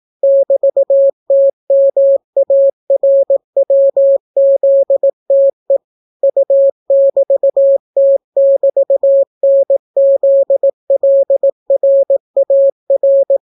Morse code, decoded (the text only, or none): <BT>TMARWZTE U<BT>T<BT>NZLRAR